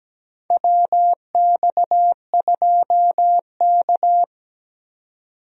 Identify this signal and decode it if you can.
Morse code, heard as a keyed tone, about 17 wpm, 700 Hz. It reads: WX2K
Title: USA Call Signs - 4 Characters - 17wpm